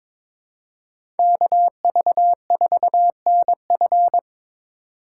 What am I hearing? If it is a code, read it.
Morse code, heard as a keyed tone, about 21 wpm, 700 Hz. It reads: KV4NF
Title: USA Call Signs - 5 Characters - 3x Character Spacing - 22wpm